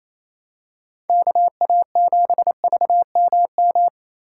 Morse code, heard as a keyed tone, about 28 wpm, 700 Hz. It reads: KA7VMM
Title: USA Call Signs - 6 Characters - 3x Character Spacing - 28wpm